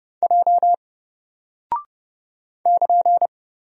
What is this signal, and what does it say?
Morse code, heard as a keyed tone, about 30 wpm, 700 Hz. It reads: J <KN>
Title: Single Character (Letter:Number:Punctuation:Prosign) 30wpm